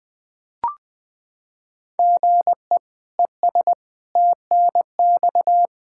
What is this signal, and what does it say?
Morse code, heard as a keyed tone, about 20 wpm, 700 Hz. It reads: GE ES TNX
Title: Practice QSO - Exchange 1 - Beginning - 20wpm